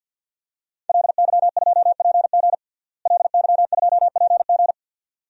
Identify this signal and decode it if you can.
Morse code, heard as a keyed tone, about 47 wpm, 700 Hz. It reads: LY2PG LY2PG
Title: DXCC-100 Call Signs - 5 Characters - Repeated 2x - 50wpm